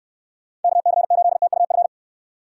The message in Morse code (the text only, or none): KC8TUF